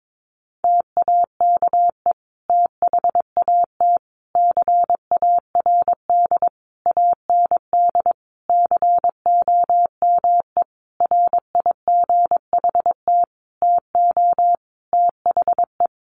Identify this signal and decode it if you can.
Morse code, heard as a keyed tone, about 22 wpm, 700 Hz. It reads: TAKE THAT CARD AND COME RIGHT TO THE
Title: Sentences from Top 1400 Words 22wpm